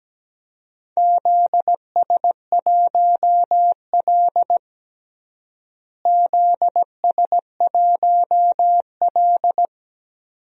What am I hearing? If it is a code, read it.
Morse code, heard as a keyed tone, about 17 wpm, 700 Hz. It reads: ZS1L ZS1L
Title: DXCC-100 Call Signs - 4 Characters - Repeated 2x - 17wpm